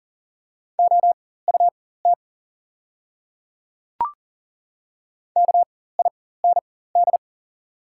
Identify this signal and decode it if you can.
Morse code, heard as a keyed tone, about 40 wpm, spaced as for 19 wpm, 700 Hz. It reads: OUT KIND
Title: 4x Character Spacing - Top 300 Words - 40wpm